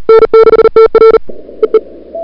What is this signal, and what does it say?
Morse code, heard as a keyed tone, about 37 wpm, 440 Hz. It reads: N6TR EE
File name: CWSprint / February 2007 CW Sprint / 01-2007 CW Sprint Hour1